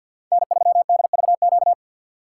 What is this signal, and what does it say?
Morse code, heard as a keyed tone, about 53 wpm, 700 Hz. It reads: N3DVQ
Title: USA Call Signs - 5 Characters - 8x Character Spacing - 50wpm